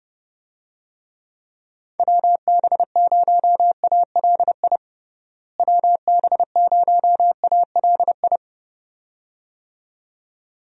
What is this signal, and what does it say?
Morse code, heard as a keyed tone, about 30 wpm, 700 Hz. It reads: WB0ALI WB0ALI
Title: USA Call Signs - 6 Characters - Repeated 2x - 30wpm